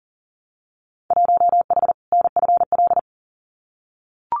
Morse code, heard as a keyed tone, about 40 wpm, 700 Hz. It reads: 1H NFL
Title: Winter Field Day Exchanges 40wpm